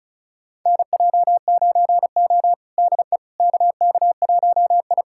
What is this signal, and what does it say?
Morse code, heard as a keyed tone, about 36 wpm, 700 Hz. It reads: NJ9O DE KK1I
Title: USA Callsign DE Callsign - 4 Characters - 35wpm